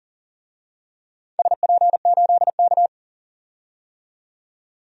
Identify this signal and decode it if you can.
Morse code, heard as a keyed tone, about 41 wpm, 700 Hz. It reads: SP8K
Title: DXCC-100 Call Signs - 4 Characters - 40wpm